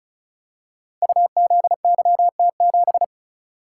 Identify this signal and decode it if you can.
Morse code, heard as a keyed tone, about 35 wpm, 700 Hz. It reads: UZYT7
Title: Code Groups - 5 Letter:Numbers - 35wpm